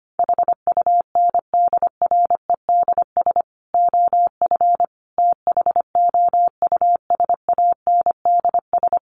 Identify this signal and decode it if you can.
Morse code, heard as a keyed tone, about 25 wpm, 700 Hz. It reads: HUNDREDS OF THOUSANDS
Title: Sets of 4 Words 25wpm